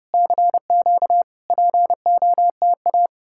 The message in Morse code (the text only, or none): CQ POTA